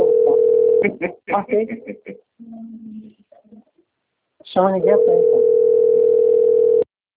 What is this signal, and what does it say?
Ringback tone (North America), 440 + 480 Hz, with speech.
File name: rg-1600-unknown- - - .67272